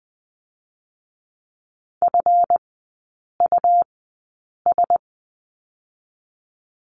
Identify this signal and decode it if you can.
Morse code, heard as a keyed tone, about 21 wpm, spaced as for 8 wpm, 700 Hz. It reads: FUS
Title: 138 Lesson 23e - Letter f - Three Characters 20wpm